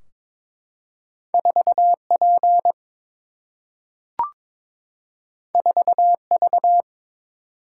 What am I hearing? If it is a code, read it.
Morse code, heard as a keyed tone, about 22 wpm, 700 Hz. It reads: 4 P 4 V
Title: Single-Letter Single-Number 22wpm